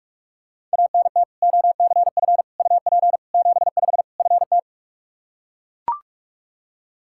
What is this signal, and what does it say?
Morse code, heard as a keyed tone, about 41 wpm, 700 Hz. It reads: ANT OCF UP 75 FT
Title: Practice QSO - Exchange 2 - Info - 45wpm